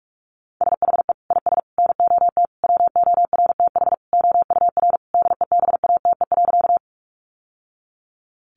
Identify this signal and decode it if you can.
Morse code, heard as a keyed tone, about 44 wpm, 700 Hz. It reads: SHE IS NOT WORTH OUR DEBATE.
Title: Sentences from Top 1100 Words 45wpm